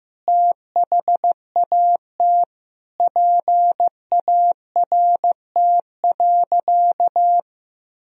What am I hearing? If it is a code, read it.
Morse code, heard as a keyed tone, about 15 wpm, 700 Hz. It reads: THAT PART.
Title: Sentences from Top 1000 Words - Review Part 1 - 15wpm